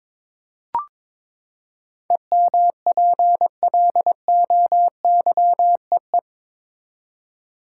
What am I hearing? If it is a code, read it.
Morse code, heard as a keyed tone, about 22 wpm, 700 Hz. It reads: EMPLOYEE
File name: practice / 22wpm / Eight Letter Words - 22wpm